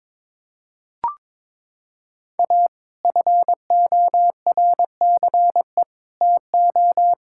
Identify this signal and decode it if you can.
Morse code, heard as a keyed tone, about 22 wpm, 700 Hz. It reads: A FORCE TO
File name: practice / 22wpm / Sets of 8 Words - 22wpm